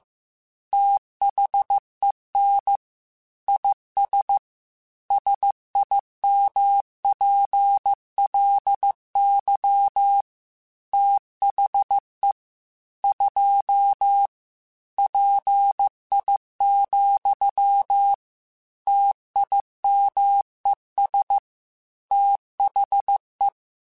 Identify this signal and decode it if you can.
Morse code, heard as a keyed tone, about 15 wpm, 800 Hz. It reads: THEN IS SIMPLY THE 2 PI, TIMES THE
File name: Technical Subjects / Technical Subjects-mp3s / XL @13